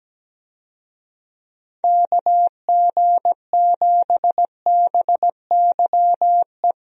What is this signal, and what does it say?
Morse code, heard as a keyed tone, about 17 wpm, 700 Hz. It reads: KG7BYE
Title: USA Call Signs - 6 Characters - Repeated 2x - 17wpm